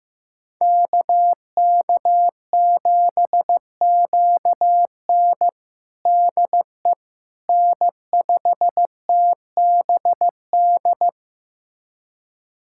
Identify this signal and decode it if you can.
Morse code, heard as a keyed tone, about 15 wpm, 700 Hz. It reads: KK7QN DE N5TBD
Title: USA Callsign DE Callsign - 5 Characters - 15wpm